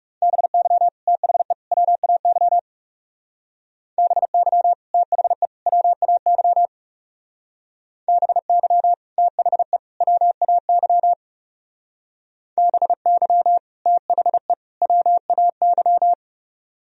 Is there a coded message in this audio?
Yes: Morse code from 0.2 to 16.1 s, heard as a keyed tone, about 39 wpm, 700 Hz. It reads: BY THE WAY BY THE WAY BY THE WAY BY THE WAY